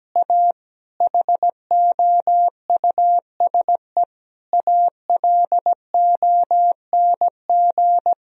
Morse code, heard as a keyed tone, about 17 wpm, 700 Hz. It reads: A HOUSE ALONG
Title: Sentences from Top 1000 Words - Review Part 2 - 17wpm